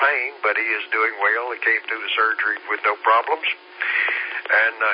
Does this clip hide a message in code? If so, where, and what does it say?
Yes: Emergency Alert System (SAME) tones from 0.5 to 4.2 s, heard as harsh buzzing bursts, with speech.